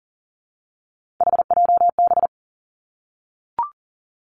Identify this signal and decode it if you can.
Morse code, heard as a keyed tone, about 42 wpm, 690 Hz. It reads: HJB
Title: Three Challenging Letter-Number Combinations 40wpm